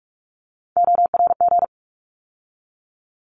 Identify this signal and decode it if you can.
Morse code, heard as a keyed tone, about 44 wpm, 700 Hz. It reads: ORG
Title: ARRL Sections 45wpm